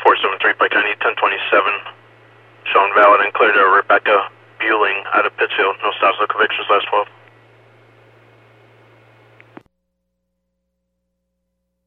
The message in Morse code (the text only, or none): none